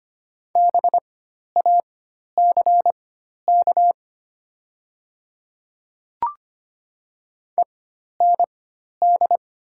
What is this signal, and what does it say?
Morse code, heard as a keyed tone, about 25 wpm, spaced as for 12 wpm, 700 Hz. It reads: BACK END
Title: 4x Character Spacing - Top 200 Words - 25wpm